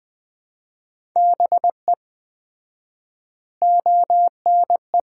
Morse code, heard as a keyed tone, about 20 wpm, 700 Hz. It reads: BE ONE
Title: 4x Word-Spacing - Sets of 4 Words - 20wpm